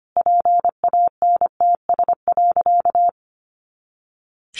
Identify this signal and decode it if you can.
Morse code, heard as a keyed tone, about 25 wpm, 700 Hz. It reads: PANTS.
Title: Additional Easy Phrases 25wpm